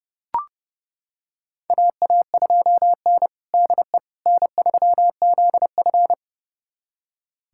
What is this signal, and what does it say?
Morse code, heard as a keyed tone, about 30 wpm, 700 Hz. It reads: AA2N DE N3ZF
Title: USA Callsign DE Callsign - 4 Characters - 30wpm